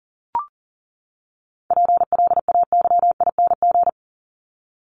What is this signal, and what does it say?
Morse code, heard as a keyed tone, about 40 wpm, 700 Hz. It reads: PLAYING
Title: ING Suffix - Top 100 Words with Base Words 40wpm